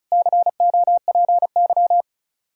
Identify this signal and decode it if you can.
Morse code, heard as a keyed tone, about 44 wpm, 700 Hz. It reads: COPY